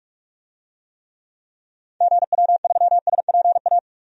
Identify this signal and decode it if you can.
Morse code, heard as a keyed tone, about 46 wpm, 700 Hz. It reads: GW3SPA
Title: DXCC-100 Call Signs - 6 Characters - 45wpm